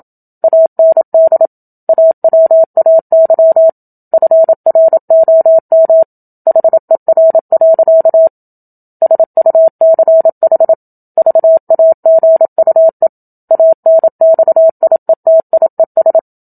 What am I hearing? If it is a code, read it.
Morse code, heard as a keyed tone, about 27 wpm, 650 Hz. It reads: AND AWAY FROM HER. SUCH VAGUE ANXIETIES
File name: Book Two / 01-Chapter 1 - 27 WPM